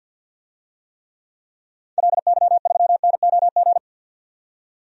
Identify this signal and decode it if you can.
Morse code, heard as a keyed tone, about 50 wpm, 700 Hz. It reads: LY3NOG
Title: DXCC-100 Call Signs - 6 Characters - 50wpm